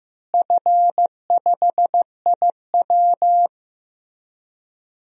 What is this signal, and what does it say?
Morse code, heard as a keyed tone, about 15 wpm, 700 Hz. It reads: F5IW
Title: DXCC-100 Call Signs - 4 Characters - Repeated 3x - 15wpm